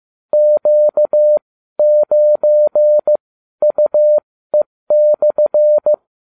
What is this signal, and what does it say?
Morse code, heard as a keyed tone, about 15 wpm, spaced as for 12 wpm, 600 Hz. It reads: Q9UE/